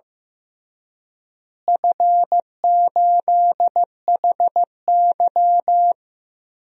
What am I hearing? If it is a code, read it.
Morse code, heard as a keyed tone, about 15 wpm, 700 Hz. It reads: F8HY
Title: DXCC-100 Call Signs - 4 Characters - Repeated 3x - 15wpm